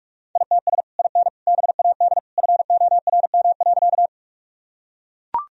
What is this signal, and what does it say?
Morse code, heard as a keyed tone, about 44 wpm, 700 Hz. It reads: ITS IN BAD FORM.
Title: Sentences from Top 1000 Words - Review Part 1 - 45wpm